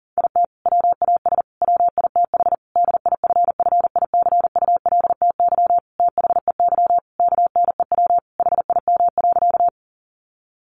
Morse code, heard as a keyed tone, about 40 wpm, 700 Hz. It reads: IT WAS WITH DIFFICULTY THEY KNEW HIM.
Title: Sentences from Top 1500 Words 40wpm